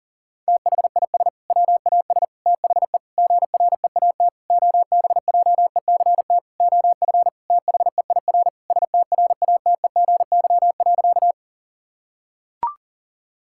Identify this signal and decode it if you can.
Morse code, heard as a keyed tone, about 40 wpm, 700 Hz. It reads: THIS WAS THE GREAT OBJECT OF THEIR STRATEGY.
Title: Sentences from Top 1000 Words - Review Part 5 - 40wpm